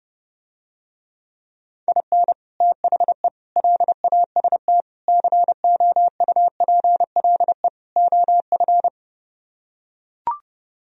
Morse code, heard as a keyed tone, about 30 wpm, 700 Hz. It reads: IN THE LAST COUPLE OF